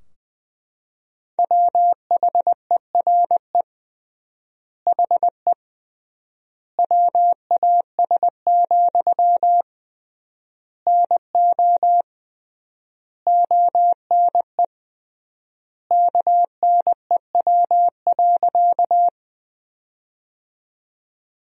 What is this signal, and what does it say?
Morse code, heard as a keyed tone, about 21 wpm, 700 Hz. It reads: WHERE HE WAS, NO ONE KNEW.
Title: 3x Word-Spacing - Sentences from Top 400 Words - 20wpm